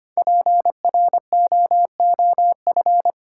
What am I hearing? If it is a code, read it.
Morse code, heard as a keyed tone, about 25 wpm, 700 Hz. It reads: PROOF
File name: 5 Letter Words - First 3 Letters Given - 25wpm